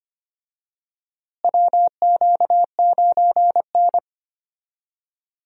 Morse code, heard as a keyed tone, about 25 wpm, 700 Hz. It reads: WQ9N
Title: USA Call Signs - 4 Characters - Repeated 2x - 25wpm